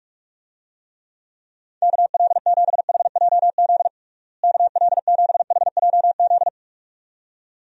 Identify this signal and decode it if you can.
Morse code, heard as a keyed tone, about 46 wpm, 700 Hz. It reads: KL7HJZ KL7HJZ
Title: DXCC-100 Call Signs - 6 Characters - Repeated 2x - 45wpm